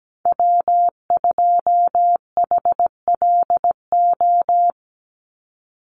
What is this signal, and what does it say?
Morse code, heard as a keyed tone, about 17 wpm, 700 Hz. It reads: W2HLO
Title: USA Call Signs 17wpm